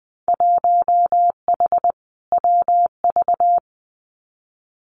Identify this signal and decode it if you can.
Morse code, heard as a keyed tone, about 20 wpm, 700 Hz. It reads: 1H WV
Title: Winter Field Day Exchanges 20wpm